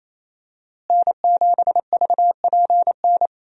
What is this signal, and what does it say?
Morse code, heard as a keyed tone, about 28 wpm, 700 Hz. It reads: N7VPN
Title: USA Call Signs - 5 Characters - Repeated 2x - 28wpm